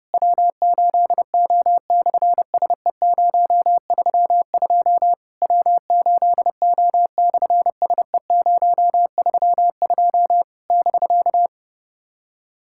Morse code, heard as a keyed tone, about 30 wpm, 700 Hz. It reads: W8O/SE032 W8O/SE032 <BK>
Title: SOTA QSO Practice 30wpm